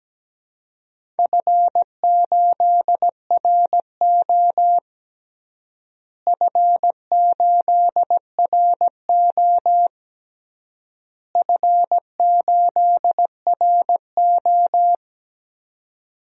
Morse code, heard as a keyed tone, about 17 wpm, 700 Hz. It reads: F8RO F8RO F8RO
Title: DXCC-100 Call Signs - 4 Characters - Repeated 3x - 17wpm